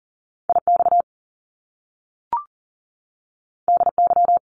Morse code, heard as a keyed tone, about 38 wpm, 700 Hz. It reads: IX DY
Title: Top Two Letter Combinations in Words 40wpm